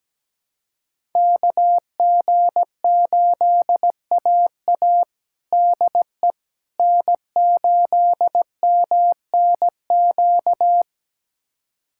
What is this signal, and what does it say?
Morse code, heard as a keyed tone, about 17 wpm, 700 Hz. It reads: KG8AA DE N8MNQ
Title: USA Callsign DE Callsign - 5 Characters - 17wpm